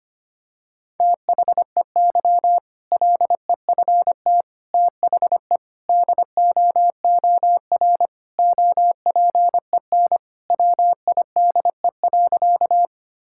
Morse code, heard as a keyed tone, about 25 wpm, 700 Hz. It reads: THEY LEFT THE DOOR OPEN WIDE.
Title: Sentences from Top 1000 Words - Review Part 6 - 25wpm